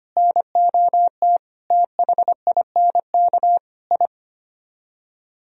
Morse code, heard as a keyed tone, about 24 wpm, 700 Hz. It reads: NOT THINK I
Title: Speed-Racing - Sets of 4 Words 25wpm